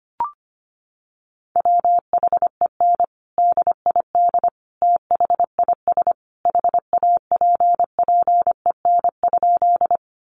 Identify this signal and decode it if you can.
Morse code, heard as a keyed tone, about 25 wpm, 700 Hz. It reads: WHEN DID THIS HAPPEN?
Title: Sentences from Top 400 Words 25wpm